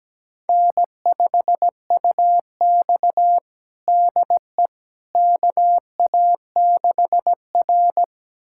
Morse code, heard as a keyed tone, about 17 wpm, 700 Hz. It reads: N5UX DE KA6R
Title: USA Callsign DE Callsign - 4 Characters - 17wpm